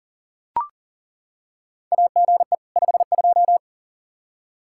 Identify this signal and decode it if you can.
Morse code, heard as a keyed tone, about 40 wpm, 700 Hz. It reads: AGE 52
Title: Practice QSO - Exchange 3 - More Info - 40wpm